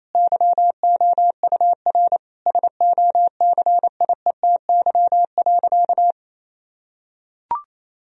Morse code, heard as a keyed tone, about 28 wpm, 700 Hz. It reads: YOUR SOCIETY.
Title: Sentences from Top 1000 Words - Review Part 4 - 28wpm